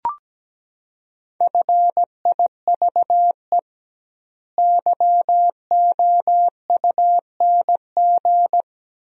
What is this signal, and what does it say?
Morse code, heard as a keyed tone, about 17 wpm, 700 Hz. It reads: FIVE YOUNG
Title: 2x Word-Spacing - Sentences from Top 400 Words - 17wpm